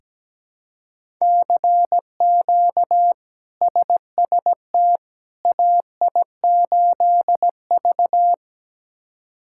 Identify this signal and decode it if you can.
Morse code, heard as a keyed tone, about 17 wpm, 700 Hz. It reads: CQ SST AI8V
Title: K1USN SST QSO Practice - 17wpm